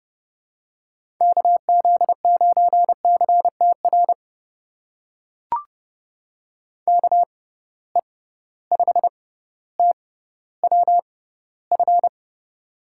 Morse code, heard as a keyed tone, about 30 wpm, spaced as for 15 wpm, 700 Hz. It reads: KZ9CTR KE5TWF
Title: USA Call Signs - 6 Characters - 6x Character Spacing - 30wpm